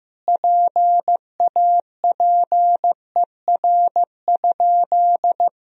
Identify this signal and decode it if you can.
Morse code, heard as a keyed tone, about 15 wpm, 700 Hz. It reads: PAPER?
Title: Sentences from Top 1000 Words - Review Part 2 - 15wpm